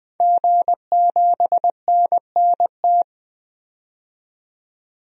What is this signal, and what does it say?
Morse code, heard as a keyed tone, about 20 wpm, 700 Hz. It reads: G7NNT
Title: DXCC-100 Call Signs - 5 Characters - 20wpm